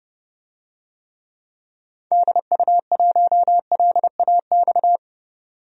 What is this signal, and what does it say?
Morse code, heard as a keyed tone, about 29 wpm, 700 Hz. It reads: DU1LAX
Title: DXCC-100 Call Signs - 6 Characters - 30wpm